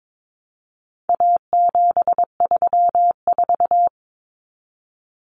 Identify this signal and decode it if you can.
Morse code, heard as a keyed tone, about 22 wpm, 700 Hz. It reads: A734
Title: Common Cut Numbers - 4-Digit 22wpm